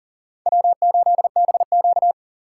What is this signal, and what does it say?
Morse code, heard as a keyed tone, about 40 wpm, 700 Hz. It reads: W8BQ
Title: USA Call Signs - 3x Character Spacing - 40wpm